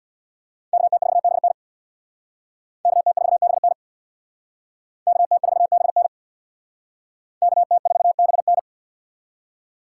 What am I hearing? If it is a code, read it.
Morse code, heard as a keyed tone, about 62 wpm, 700 Hz. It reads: KT4BN KT4BN KT4BN KT4BN
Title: Speed Racing - USA Call Signs 50wpm